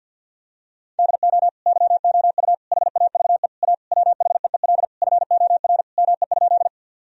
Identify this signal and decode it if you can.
Morse code, heard as a keyed tone, about 53 wpm, 700 Hz. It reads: DO YOU HAVE A WHEEL FOR ME?
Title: Sentences from Top 1000 Words - Review Part 3 - 50wpm